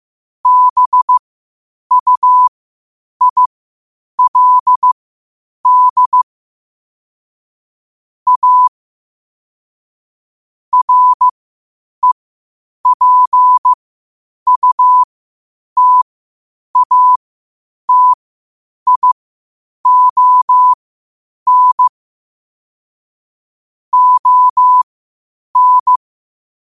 Morse code, heard as a keyed tone, about 15 wpm, spaced as for 8 wpm, 1000 Hz. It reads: BUILD A REPUTATION ON
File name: qotdfiles / 8wpm